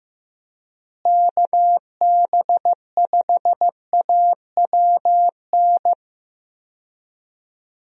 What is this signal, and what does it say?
Morse code, heard as a keyed tone, about 15 wpm, 700 Hz. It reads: KB5AWN
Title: USA Call Signs - 6 Characters - Repeated 3x - 15wpm